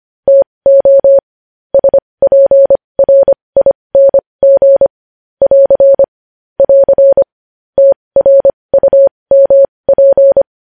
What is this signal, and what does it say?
Morse code, heard as a keyed tone, about 25 wpm, spaced as for 20 wpm, 550 Hz. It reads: TO SPRING <AR> <AR> TRUMP
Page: News Headlines in Morse Code at 20 WPM